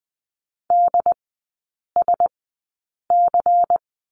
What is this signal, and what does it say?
Morse code, heard as a keyed tone, about 20 wpm, spaced as for 8 wpm, 700 Hz. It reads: DSC
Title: 093 Lesson 16e - Letter c - Three Characters 20wpm